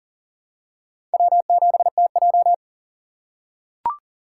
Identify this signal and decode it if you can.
Morse code, heard as a keyed tone, about 40 wpm, 700 Hz. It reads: W7TJ
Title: USA Call Signs - 2x Character Spacing - 40wpm